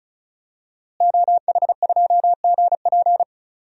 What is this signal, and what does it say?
Morse code, heard as a keyed tone, about 36 wpm, 700 Hz. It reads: OH2GP
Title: DXCC-100 Call Signs - 5 Characters - Repeated 2x - 35wpm